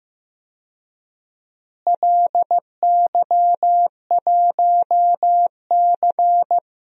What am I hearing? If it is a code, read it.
Morse code, heard as a keyed tone, about 15 wpm, 700 Hz. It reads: LY1C
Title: DXCC-100 Call Signs - 4 Characters - Repeated 3x - 15wpm